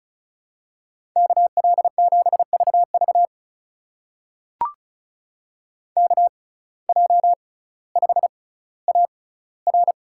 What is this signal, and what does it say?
Morse code, heard as a keyed tone, about 36 wpm, spaced as for 18 wpm, 700 Hz. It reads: KL7VV KJ5AR
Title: USA Call Signs - 5 Characters - 6x Character Spacing - 35wpm